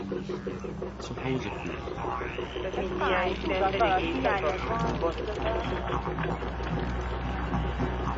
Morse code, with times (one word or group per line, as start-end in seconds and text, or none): none